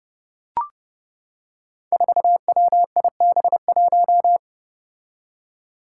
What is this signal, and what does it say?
Morse code, heard as a keyed tone, about 30 wpm, 700 Hz. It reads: 4WIB1
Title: Five Challenging Letter-Number Combinations 30wpm